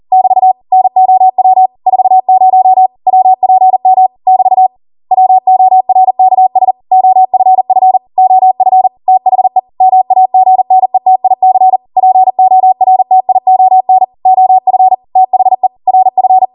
Morse code, heard as a keyed tone, about 40 wpm, 750 Hz. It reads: <BT> NOW 40 WPM <BT> WORKS OFF OF THE MAGNETIC PORTION OF THE RF
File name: Morse / 40WPM